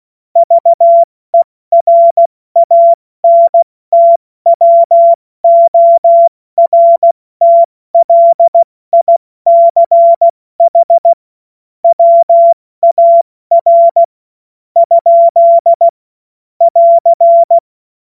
Morse code, heard as a keyed tone, about 16 wpm, spaced as for 13 wpm, 680 Hz. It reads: VERANTWORTLICH WAR? <AR>
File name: Morsekurs / Witze / Witz 21.November 2023 70 F80